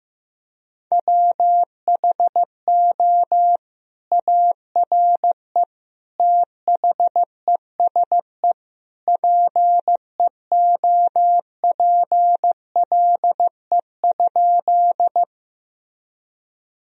Morse code, heard as a keyed tone, about 14 wpm, 700 Hz. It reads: WHO ARE THESE PEOPLE?
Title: Sentences from Top 1000 Words - Review Part 1 - 15wpm